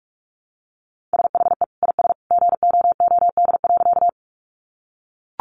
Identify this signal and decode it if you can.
Morse code, heard as a keyed tone, about 44 wpm, 700 Hz. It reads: SHE IS GOOD.
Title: Easy Phrases 45wpm